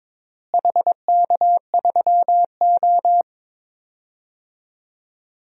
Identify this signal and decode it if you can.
Morse code, heard as a keyed tone, about 22 wpm, 700 Hz. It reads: HK3O
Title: DXCC-100 Call Signs - 4 Characters - Repeated 3x - 22wpm